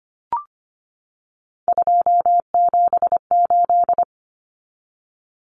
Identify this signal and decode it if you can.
Morse code, heard as a keyed tone, about 25 wpm, 700 Hz. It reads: 278
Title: Common Cut Numbers - 3-Digit 25wpm